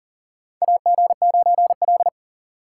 Morse code, heard as a keyed tone, about 40 wpm, 700 Hz. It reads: AG9L
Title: USA Call Signs - 4 Characters - 6x Character Spacing - 40wpm